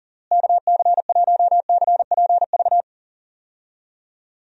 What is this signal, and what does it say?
Morse code, heard as a keyed tone, about 40 wpm, 700 Hz. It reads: KC1CPV
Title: USA Call Signs - 6 Characters - Repeated 3x - 40wpm